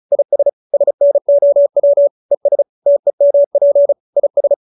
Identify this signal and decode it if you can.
Morse code, heard as a keyed tone, about 34 wpm, 570 Hz. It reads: IS SNOW ES TEMP IS
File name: QSO313 35